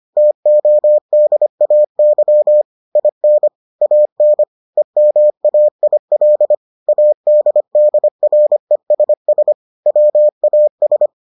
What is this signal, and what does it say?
Morse code, heard as a keyed tone, about 25 wpm, 600 Hz. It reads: TODAY IN AN EMAIL ADDRESS WAS